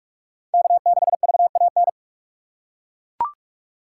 Morse code, heard as a keyed tone, about 42 wpm, 700 Hz. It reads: K 6 V A N
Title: USA Call Signs - 5 Characters - 6x Character Spacing - 45wpm